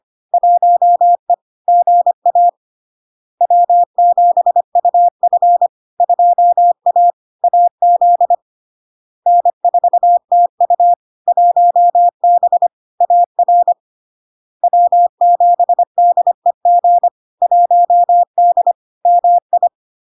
Morse code, heard as a keyed tone, about 25 wpm, 700 Hz. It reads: 1E GA W7UF 2A AZ N4TU 1B AR W7DEG 1D MI
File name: LARK-FD-2022-Log-in-CW-25wpm-700-Hz